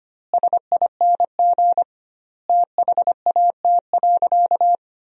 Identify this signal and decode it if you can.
Morse code, heard as a keyed tone, about 25 wpm, 700 Hz. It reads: SING THAT.
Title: 2x Word-Spacing - Sentences from Top 500 Words - 25wpm